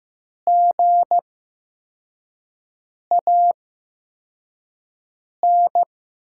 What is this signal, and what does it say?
Morse code, heard as a keyed tone, about 15 wpm, spaced as for 4 wpm, 700 Hz. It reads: GAN
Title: 8x Character Spacing - Top 500 Words - 15wpm